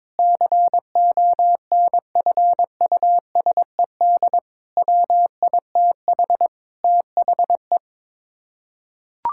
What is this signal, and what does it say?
Morse code, heard as a keyed tone, about 22 wpm, 700 Hz. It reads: CONFUSED WITH THE
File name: practice / 22wpm / Sets of 7 Words - 22wpm